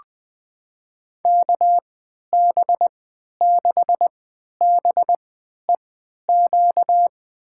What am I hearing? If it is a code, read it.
Morse code, heard as a keyed tone, about 20 wpm, spaced as for 11 wpm, 700 Hz. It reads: KB6BEQ
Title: USA Call Signs - 6 Characters - 3x Character Spacing - 20wpm